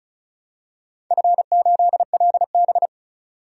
Morse code, heard as a keyed tone, about 34 wpm, 700 Hz. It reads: F8LB
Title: DXCC-100 Call Signs - 4 Characters - 35wpm